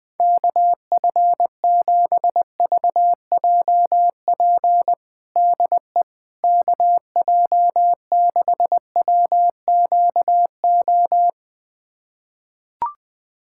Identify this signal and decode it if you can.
Morse code, heard as a keyed tone, about 20 wpm, 700 Hz. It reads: KF7VJP DE KJ6WQO
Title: USA Callsign DE Callsign 20wpm